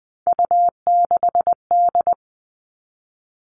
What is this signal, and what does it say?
Morse code, heard as a keyed tone, about 20 wpm, 700 Hz. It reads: U6D
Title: All Cut Numbers - 3-Digit 20wpm